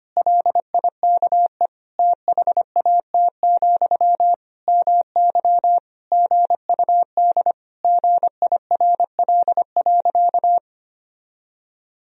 Morse code, heard as a keyed tone, about 25 wpm, 700 Hz. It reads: LIKE THAT, MY GUD GIRL.
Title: Sentences from Top 400 Words and Abbreviations 25wpm